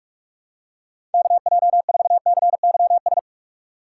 Morse code, heard as a keyed tone, about 44 wpm, 700 Hz. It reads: KJ4CYS
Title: USA Call Signs - 6 Characters - 6x Character Spacing - 45wpm